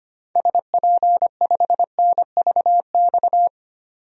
Morse code, heard as a keyed tone, about 25 wpm, 700 Hz. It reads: SP5NVX
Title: DXCC-100 Call Signs - 6 Characters - 25wpm